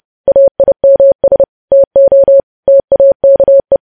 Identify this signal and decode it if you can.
Morse code, heard as a keyed tone, about 30 wpm, 550 Hz. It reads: AIMS TO TAKE
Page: news headlines in morse code at 30 wpm